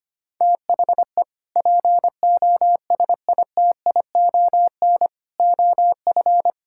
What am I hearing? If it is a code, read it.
Morse code, heard as a keyed tone, about 25 wpm, 700 Hz. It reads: THE POSITION OF
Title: Sentences from Top 1600 Words 25wpm